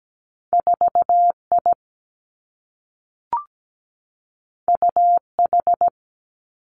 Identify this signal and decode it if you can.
Morse code, heard as a keyed tone, about 17 wpm, 700 Hz. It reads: 4 I U H
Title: Two Challenging Letter-Number Combinations 17wpm